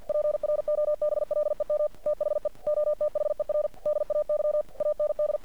Morse code, heard as a keyed tone, about 50 wpm, 600 Hz. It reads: PROBLEM THE OTHER DAY AND
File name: archived / CRSnet